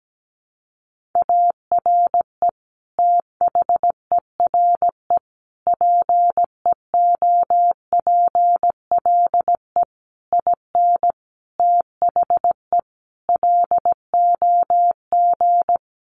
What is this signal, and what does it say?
Morse code, heard as a keyed tone, about 17 wpm, 700 Hz. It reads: ARE THERE PEOPLE IN THE LOG
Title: Sentences from Top 1400 Words 17wpm